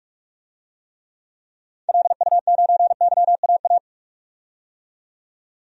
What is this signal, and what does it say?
Morse code, heard as a keyed tone, about 46 wpm, 700 Hz. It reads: LU9YAA
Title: DXCC-100 Call Signs - 6 Characters - 45wpm